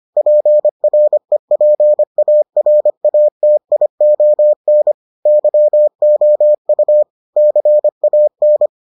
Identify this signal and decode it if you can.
Morse code, heard as a keyed tone, about 25 wpm, 600 Hz. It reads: PREPARATION YOU CAN